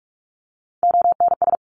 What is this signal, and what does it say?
Morse code, heard as a keyed tone, about 44 wpm, 700 Hz. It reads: ONS
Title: ARRL Sections 45wpm